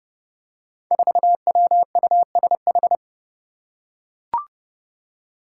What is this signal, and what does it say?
Morse code, heard as a keyed tone, about 30 wpm, 700 Hz. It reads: 4WUSH
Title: Five Challenging Letter-Number Combinations 30wpm